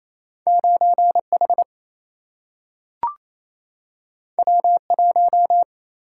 Morse code, heard as a keyed tone, about 28 wpm, 700 Hz. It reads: 9H W1
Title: Single-Letter Single-Number 28wpm